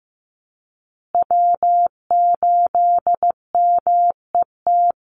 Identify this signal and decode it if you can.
Morse code, heard as a keyed tone, about 15 wpm, 700 Hz. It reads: W8MET
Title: Call Signs Repeated 2x - 15wpm